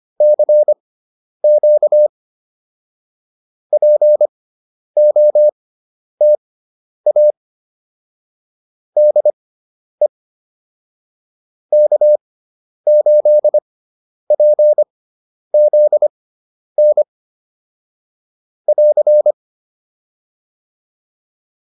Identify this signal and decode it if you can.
Morse code, heard as a keyed tone, about 25 wpm, spaced as for 10 wpm, 600 Hz. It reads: CQ POTA DE K8PZN <AR>
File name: POTA201 10